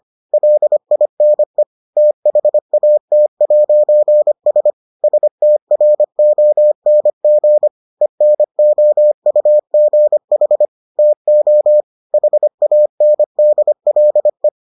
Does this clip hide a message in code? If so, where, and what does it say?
Yes: Morse code from 0.3 to 14.5 s, heard as a keyed tone, about 25 wpm, 600 Hz. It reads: LINE THAT'S STRONG ENOUGH TO HANDLE